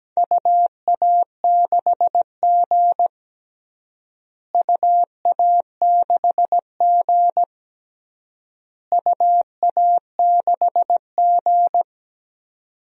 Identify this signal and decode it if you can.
Morse code, heard as a keyed tone, about 17 wpm, 700 Hz. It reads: UA6G UA6G UA6G
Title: DXCC-100 Call Signs - 4 Characters - Repeated 3x - 17wpm